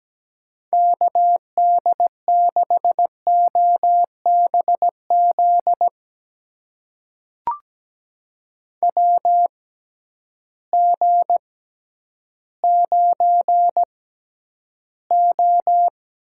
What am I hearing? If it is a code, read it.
Morse code, heard as a keyed tone, about 17 wpm, spaced as for 10 wpm, 700 Hz. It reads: KD6OBZ WG9O
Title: USA Call Signs - 6x Character Spacing - 17wpm